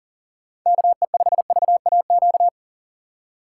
Morse code, heard as a keyed tone, about 40 wpm, 700 Hz. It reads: KE5VAQ
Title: USA Call Signs - 6 Characters - 2x Character Spacing - 40wpm